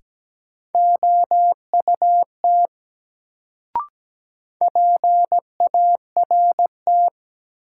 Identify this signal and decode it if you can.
Morse code, heard as a keyed tone, about 17 wpm, 700 Hz. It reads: OUT PART
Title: Sets of 2 Words - New Word Drill - 17wpm